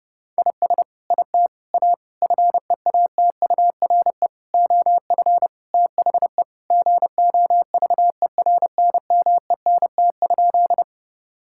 Morse code, heard as a keyed tone, about 32 wpm, 700 Hz. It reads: IS IT A FEATURE OF THE GOVERNMENT?
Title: Sentences from Top 1000 Words - Review Part 5 - 30wpm